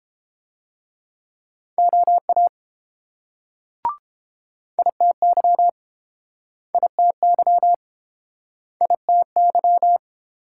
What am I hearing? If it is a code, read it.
Morse code, heard as a keyed tone, about 29 wpm, 700 Hz. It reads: OA ITY ITY ITY